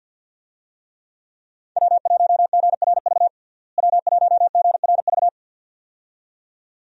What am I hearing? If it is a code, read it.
Morse code, heard as a keyed tone, about 51 wpm, 700 Hz. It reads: W1GRV W1GRV
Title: USA Call Signs - 5 Characters - Repeated 2x - 50wpm